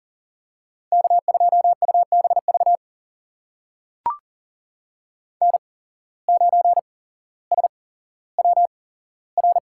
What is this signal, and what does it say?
Morse code, heard as a keyed tone, about 40 wpm, spaced as for 17 wpm, 700 Hz. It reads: K2UBV N9SWR
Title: USA Call Signs - 5 Characters - 8x Character Spacing - 40wpm